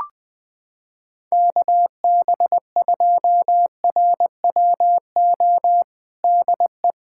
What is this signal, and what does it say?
Morse code, heard as a keyed tone, about 20 wpm, 700 Hz. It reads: KB2RWO DE
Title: USA Callsign DE Callsign - 6 Characters - 20wpm